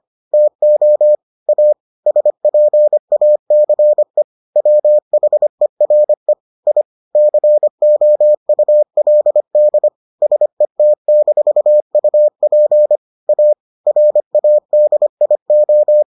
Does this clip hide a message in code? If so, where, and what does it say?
Yes: Morse code from 0.3 to 16.0 s, heard as a keyed tone, about 25 wpm, 600 Hz. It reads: TO A SPACE WHERE I COULD SET-UP A RADIO